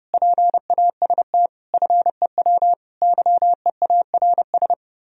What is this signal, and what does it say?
Morse code, heard as a keyed tone, about 36 wpm, 700 Hz. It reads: PAST FEW YEARS
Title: Speed-Racing - Sets of 5 Words 30wpm